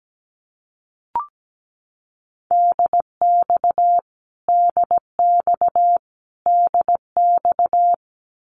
Morse code, heard as a keyed tone, about 17 wpm, 700 Hz. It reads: DX DX DX
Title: Abbreviations Repeated 3x - 17wpm